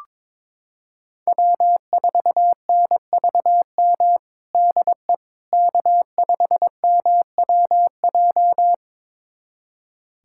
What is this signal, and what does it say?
Morse code, heard as a keyed tone, about 22 wpm, 700 Hz. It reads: W4NVM DE K5MWJ
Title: USA Callsign DE Callsign - 5 Characters - 22wpm